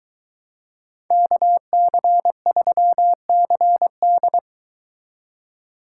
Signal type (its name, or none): Morse code